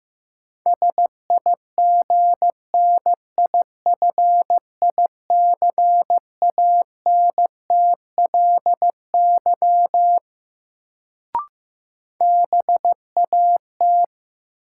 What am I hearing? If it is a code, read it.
Morse code, heard as a keyed tone, about 16 wpm, 700 Hz. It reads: SIGNIFICANTLY BAT
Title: Sentences from Top 1500 Words - New Word Drill - 15wpm